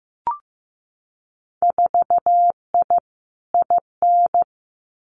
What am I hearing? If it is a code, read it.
Morse code, heard as a keyed tone, about 15 wpm, 700 Hz. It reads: 4I IN
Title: Winter Field Day Exchanges Encore 15wpm